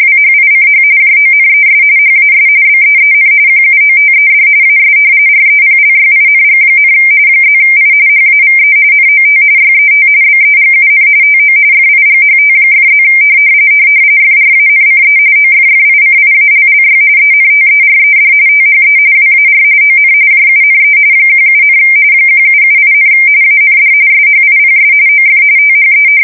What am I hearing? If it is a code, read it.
RTTY, about 45.45 baud. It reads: WHICH JONES HAS BEEN QUOTED AS REPLYING, +I HAVE NOT YET BEGUN TO FIGHT!+ FRANCE OFFICIALLY ENTERED THE WAR ON JUNE 17, 1778, AND SHIPS OF THE
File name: fsk170 / usnavy1